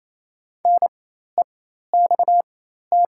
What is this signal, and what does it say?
Morse code, heard as a keyed tone, about 28 wpm, spaced as for 13 wpm, 700 Hz. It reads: NEXT
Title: 4x Character Spacing - Top 500 Words - 28wpm